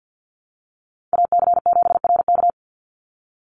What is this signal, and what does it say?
Morse code, heard as a keyed tone, about 50 wpm, 700 Hz. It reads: AC7RK
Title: USA Call Signs 50wpm